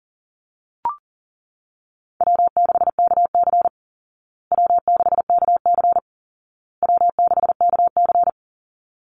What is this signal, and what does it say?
Morse code, heard as a keyed tone, about 40 wpm, 700 Hz. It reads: W6KC W6KC W6KC
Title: Easy Call Signs Repeated 3x - 40wpm